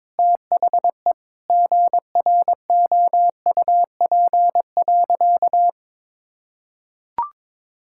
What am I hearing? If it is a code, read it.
Morse code, heard as a keyed tone, about 22 wpm, 700 Hz. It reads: THE GROUP.
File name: practice / 22wpm / Sentences with 2-Word Phrases - 22wpm